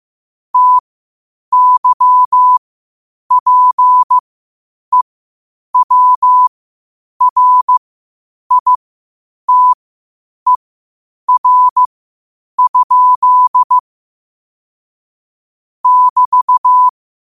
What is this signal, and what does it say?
Morse code, heard as a keyed tone, about 15 wpm, spaced as for 8 wpm, 1000 Hz. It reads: TYPEWRITER? <BT>
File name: qotdfiles / 8wpm